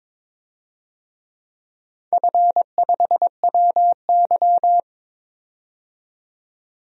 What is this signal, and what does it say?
Morse code, heard as a keyed tone, about 22 wpm, 700 Hz. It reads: F5WY
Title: DXCC-100 Call Signs - 4 Characters - 22wpm